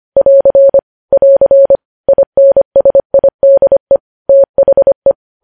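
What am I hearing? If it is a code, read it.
Morse code, heard as a keyed tone, about 25 wpm, 550 Hz. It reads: <AR> <AR> INSIDE THE
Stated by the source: Morse code practice of news headlines at 25 wpm, opening with vvv.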